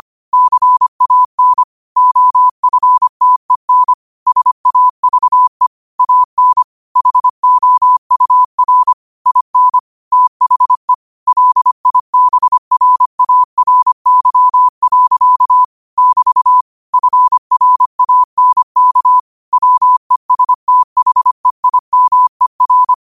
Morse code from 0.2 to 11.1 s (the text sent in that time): CAN OFTEN SAVE AN HOUR IN THE